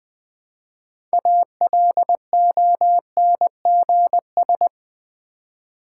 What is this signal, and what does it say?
Morse code, heard as a keyed tone, about 20 wpm, 700 Hz. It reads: ALONGS
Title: Build A Word - 3 Letters At A Time - 9-Letter Words - 20wpm